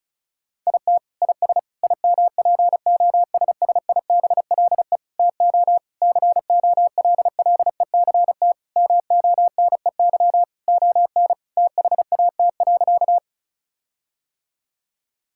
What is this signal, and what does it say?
Morse code, heard as a keyed tone, about 36 wpm, 700 Hz. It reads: IT IS IMPOSSIBLE TO COLLECT MONEY ON THAT.
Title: Sentences from Top 1600 Words 35wpm